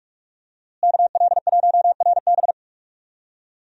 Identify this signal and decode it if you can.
Morse code, heard as a keyed tone, about 46 wpm, 700 Hz. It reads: KL1RB
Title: USA Call Signs - 5 Characters - Repeated 3x - 45wpm